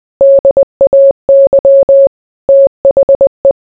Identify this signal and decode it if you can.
Morse code, heard as a keyed tone, about 20 wpm, 550 Hz. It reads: DAY THE